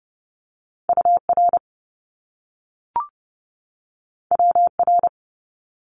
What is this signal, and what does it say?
Morse code, heard as a keyed tone, about 30 wpm, 700 Hz. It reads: U R W R
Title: Top Two Letter Combinations in Words 30wpm